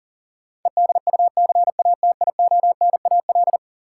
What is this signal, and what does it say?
Morse code, heard as a keyed tone, about 40 wpm, 700 Hz. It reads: EDUCATIONAL
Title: Sentences from Top 1200 Words - New Word Drill Alt - 40wpm